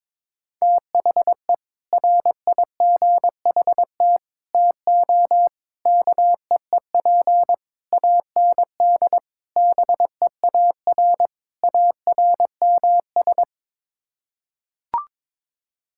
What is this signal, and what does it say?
Morse code, heard as a keyed tone, about 22 wpm, 700 Hz. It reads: THE RIGHT TO KEEP AND BEAR ARMS
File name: practice / 22wpm / Sets of 7 Words - 22wpm